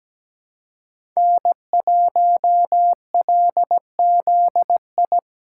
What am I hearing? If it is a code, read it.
Morse code, heard as a keyed tone, about 17 wpm, 700 Hz. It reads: N1LZI
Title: USA Call Signs - 5 Characters - 3x Character Spacing - 17wpm